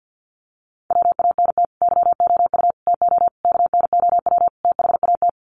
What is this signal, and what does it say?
Morse code, heard as a keyed tone, about 50 wpm, 700 Hz. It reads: WANT YOU TO KNOW THAT
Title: Sets of 5 Words Encore 50wpm